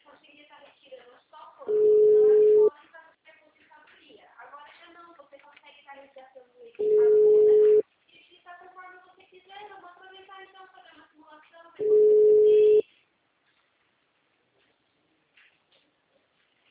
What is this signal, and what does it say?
Ringback tone (Europe, 425 Hz), with speech.